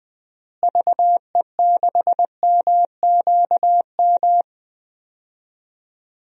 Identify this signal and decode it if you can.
Morse code, heard as a keyed tone, about 20 wpm, 700 Hz. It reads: VE6MQM
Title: DXCC-100 Call Signs - 6 Characters - Repeated 2x - 20wpm